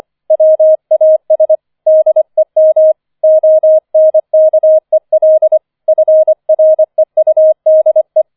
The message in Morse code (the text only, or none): WAS DEM ONKEL FREUDE